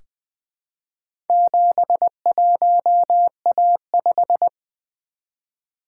Morse code, heard as a keyed tone, about 20 wpm, 700 Hz. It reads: 71A5
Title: Four Letter-Number Combinations - 20wpm